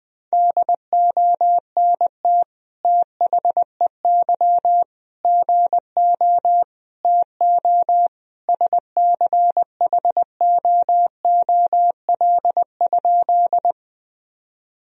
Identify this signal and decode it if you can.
Morse code, heard as a keyed tone, about 20 wpm, 700 Hz. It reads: DONT THEY GO TO SCHOOL?
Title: Sentences from Top 1000 Words - Review Part 1 - 20wpm